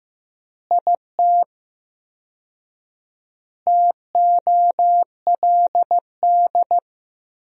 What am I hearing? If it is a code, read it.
Morse code, heard as a keyed tone, about 15 wpm, 700 Hz. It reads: IT TOLD
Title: 4x Word-Spacing - Sentences from Top 400 Words - 15wpm